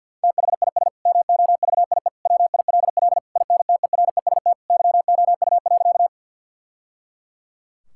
Morse code, heard as a keyed tone, about 50 wpm, 700 Hz. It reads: THIS MOVIE WILL INTEREST YOU.
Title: Sentences from Top 1000 Words - Review Part 5 - 50wpm